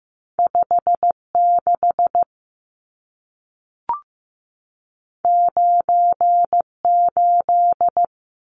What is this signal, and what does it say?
Morse code, heard as a keyed tone, about 15 wpm, 700 Hz. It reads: 56 98
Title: Numbers - 2-Digit 15wpm